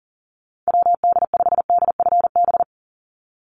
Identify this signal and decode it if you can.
Morse code, heard as a keyed tone, about 40 wpm, 700 Hz. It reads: WD5DFB
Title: USA Call Signs 40wpm